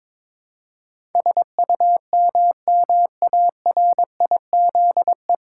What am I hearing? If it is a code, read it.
Morse code, heard as a keyed tone, about 22 wpm, 700 Hz. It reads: SUMMARIZE
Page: Morse Code Ninja practice 22wpm